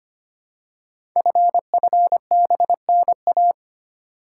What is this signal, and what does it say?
Morse code, heard as a keyed tone, about 25 wpm, 700 Hz. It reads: FFBNA
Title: Code Groups - 5 Letters - 25wpm